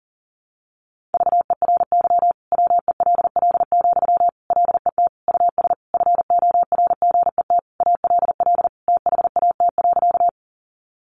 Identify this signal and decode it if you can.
Morse code, heard as a keyed tone, about 40 wpm, 700 Hz. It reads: VERY WELL, LET US FORGET ALL THAT.
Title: Sentences from Top 1200 Words 40wpm